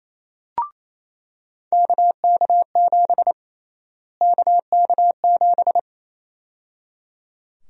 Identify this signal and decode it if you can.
Morse code, heard as a keyed tone, about 28 wpm, 700 Hz. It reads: KK7 KK7
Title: USA Call Sign Prefixes - Repeated 2x - 28wpm